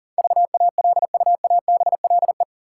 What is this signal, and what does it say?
Morse code, heard as a keyed tone, about 40 wpm, 700 Hz. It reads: VALUABLE
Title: Sentences from Top 1100 Words - New Word Drill - 40wpm